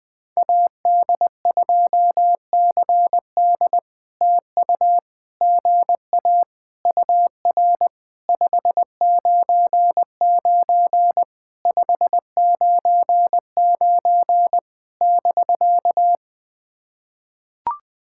Morse code, heard as a keyed tone, about 21 wpm, 700 Hz. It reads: AD2CD TU GA UR 599 599 <BK>
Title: POTA QSO Practice - 20wpm